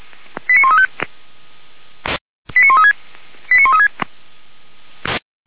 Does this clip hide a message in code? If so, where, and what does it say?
Yes: five-tone selective call (ZVEI) from 0.5 to 3.9 s, heard as quick runs of five tones, tones of about 80 ms. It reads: 89135 89136 89136